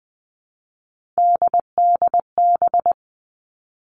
Morse code, heard as a keyed tone, about 20 wpm, 700 Hz. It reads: DDB
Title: All Cut Numbers - 3-Digit 20wpm